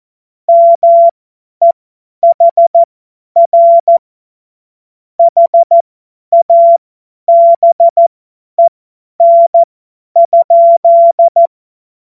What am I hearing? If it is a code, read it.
Morse code, heard as a keyed tone, about 14 wpm, spaced as for 10 wpm, 680 Hz. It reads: MEHR HABEN?
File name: Morsekurs / Witze / Witz 02. Mai 2023 50 F70